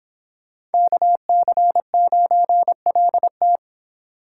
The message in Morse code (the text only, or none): KC9LT